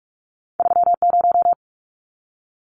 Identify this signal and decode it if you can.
Morse code, heard as a keyed tone, about 46 wpm, 700 Hz. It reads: 30
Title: Numbers - 2-Digit 45wpm